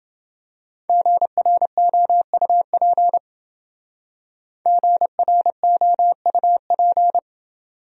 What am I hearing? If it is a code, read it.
Morse code, heard as a keyed tone, about 29 wpm, 700 Hz. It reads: GROUP GROUP